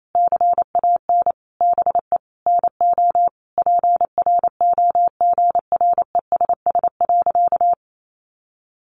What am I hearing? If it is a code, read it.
Morse code, heard as a keyed tone, about 28 wpm, 700 Hz. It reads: CAN BE NO PROGRESS.
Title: Sentences from Top 1400 Words 28wpm